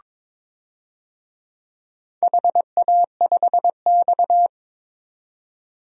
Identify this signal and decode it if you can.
Morse code, heard as a keyed tone, about 22 wpm, 700 Hz. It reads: HA5X
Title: DXCC-100 Call Signs - 4 Characters - 22wpm